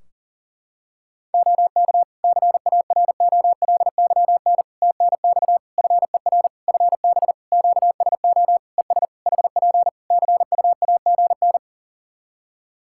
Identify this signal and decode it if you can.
Morse code, heard as a keyed tone, about 40 wpm, 700 Hz. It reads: OK CAROLYN TNX FER FB QSO ES HP CUAGN
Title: Practice QSO - Exchange 4 - Ending - 40wpm